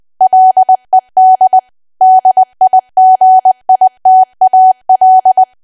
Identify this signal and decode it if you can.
Morse code, heard as a keyed tone, about 20 wpm, 750 Hz. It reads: LED DIGITAL
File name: cwa / W1AW-Level3 / 20WPM